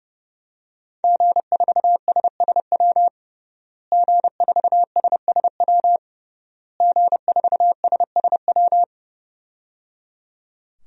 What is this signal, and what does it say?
Morse code, heard as a keyed tone, about 29 wpm, 700 Hz. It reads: G4SSW G4SSW G4SSW
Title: DXCC-100 Call Signs - 5 Characters - Repeated 3x - 30wpm